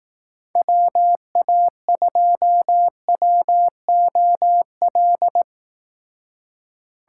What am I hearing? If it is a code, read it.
Morse code, heard as a keyed tone, about 18 wpm, 700 Hz. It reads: WA2WOL
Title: Speed Racing - USA 2x3 Call Signs 15wpm